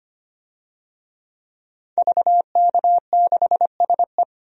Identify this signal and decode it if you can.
Morse code, heard as a keyed tone, about 25 wpm, 700 Hz. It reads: VK6SE
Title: DXCC-100 Call Signs - 5 Characters - 25wpm